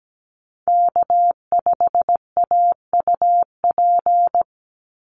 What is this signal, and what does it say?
Morse code, heard as a keyed tone, about 17 wpm, 700 Hz. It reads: K5AUP
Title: Call Signs Repeated 2x - 17wpm